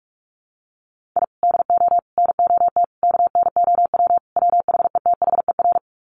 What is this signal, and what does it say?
Morse code, heard as a keyed tone, about 44 wpm, 700 Hz. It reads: I DO NOT KNOW WHETHER
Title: Sets of 5 Words 45wpm